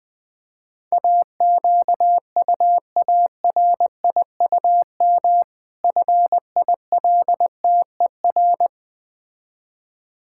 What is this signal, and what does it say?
Morse code, heard as a keyed tone, about 20 wpm, 700 Hz. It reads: AQUARIUM FILTER
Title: Difficult Words with Hints - 20wpm